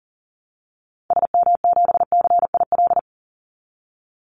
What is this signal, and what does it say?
Morse code, heard as a keyed tone, about 40 wpm, 700 Hz. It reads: SM7CIL
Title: Call Signs Repeated 2x - 40wpm